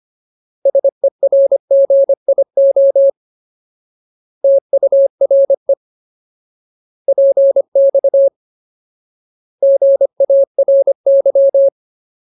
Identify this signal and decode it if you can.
Morse code, heard as a keyed tone, about 25 wpm, 550 Hz. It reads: SERGIO TURE PX GARY
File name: Ultimate Word List / 25WPM Ultimate